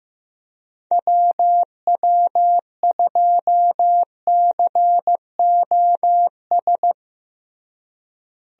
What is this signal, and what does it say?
Morse code, heard as a keyed tone, about 15 wpm, 700 Hz. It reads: WW2COS
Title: USA Call Signs - 6 Characters - Repeated 3x - 15wpm